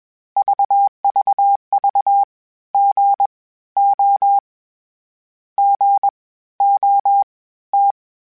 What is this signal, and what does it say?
Morse code, heard as a keyed tone, about 21 wpm, spaced as for 14 wpm, 800 Hz. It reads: VVV GO GOT
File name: Short Runs / Short Runs-mp3s / Gotten @20